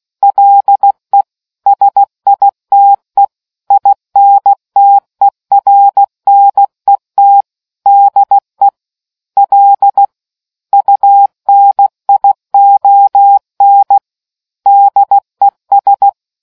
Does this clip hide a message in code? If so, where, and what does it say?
Yes: Morse code from 0.2 to 16.1 s, heard as a keyed tone, about 16 wpm, 800 Hz. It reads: LE SITE INTERNET DE L UNION DES